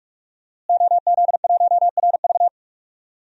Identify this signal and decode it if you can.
Morse code, heard as a keyed tone, about 46 wpm, 700 Hz. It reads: OZ1RV
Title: DXCC-100 Call Signs - 5 Characters - Repeated 2x - 45wpm